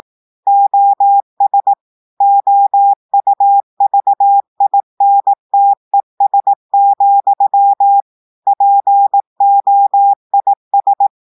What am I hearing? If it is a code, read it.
Morse code, heard as a keyed tone, about 18 wpm, 800 Hz. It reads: OS OUVINTES, POIS
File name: portugais texte en clair 2 vitesse 18 mots minutes